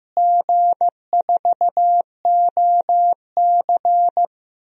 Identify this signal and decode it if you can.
Morse code, heard as a keyed tone, about 15 wpm, 700 Hz. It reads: G4OC
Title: DXCC-100 Call Signs - 4 Characters - Repeated 3x - 15wpm